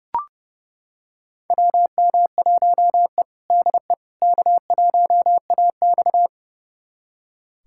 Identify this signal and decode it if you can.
Morse code, heard as a keyed tone, about 30 wpm, 700 Hz. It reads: WM1E DE K1AX
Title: USA Callsign DE Callsign - 4 Characters - 30wpm